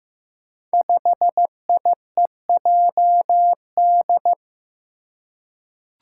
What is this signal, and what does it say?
Morse code, heard as a keyed tone, about 15 wpm, 700 Hz. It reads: 5IEJD
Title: Five Challenging Letter-Number Combinations 15wpm